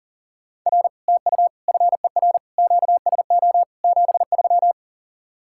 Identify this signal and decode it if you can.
Morse code, heard as a keyed tone, about 52 wpm, 700 Hz. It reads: R TU FER QSO 73